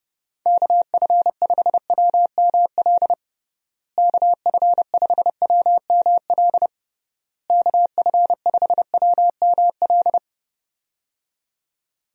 Morse code, heard as a keyed tone, about 30 wpm, 700 Hz. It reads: KF5WML KF5WML KF5WML
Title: USA Call Signs - 6 Characters - Repeated 3x - 30wpm